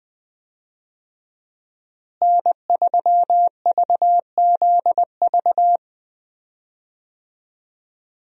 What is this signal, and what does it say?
Morse code, heard as a keyed tone, about 20 wpm, 700 Hz. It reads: N3VZV
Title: USA Call Signs - 5 Characters - Repeated 2x - 20wpm